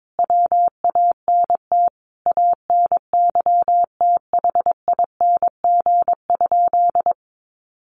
Morse code, heard as a keyed tone, about 22 wpm, 700 Hz. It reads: WANT ANYTHING?
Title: Sentences from Top 1000 Words 22wpm